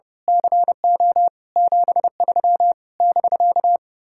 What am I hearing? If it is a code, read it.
Morse code, heard as a keyed tone, about 30 wpm, 700 Hz. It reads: CO 73 <BK>
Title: Speed-Racing Common QSO Phrases - 30wpm